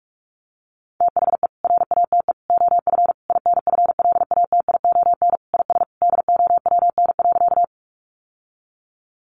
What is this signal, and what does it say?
Morse code, heard as a keyed tone, about 44 wpm, 700 Hz. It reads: THE RATE OF INFLATION IS DOWN.
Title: Sentences from Top 1100 Words 45wpm